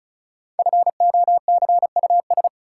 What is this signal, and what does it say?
Morse code, heard as a keyed tone, about 36 wpm, 700 Hz. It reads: FOCUS
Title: Sentences from Top 1000 Words - Review Part 4 - New Word Drill - 35wpm